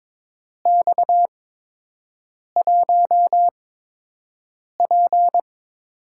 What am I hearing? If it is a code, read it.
Morse code, heard as a keyed tone, about 22 wpm, spaced as for 6 wpm, 700 Hz. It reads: X1P
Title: USA Call Signs - 4 Characters - 8x Character Spacing - 22wpm